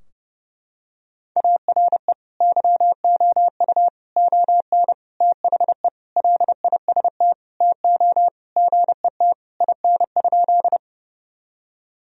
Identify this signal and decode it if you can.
Morse code, heard as a keyed tone, about 30 wpm, 700 Hz. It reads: ARE YOU ON THE LIST TO GET IN?
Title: Sentences from Top 1000 Words - Review Part 2 - 30wpm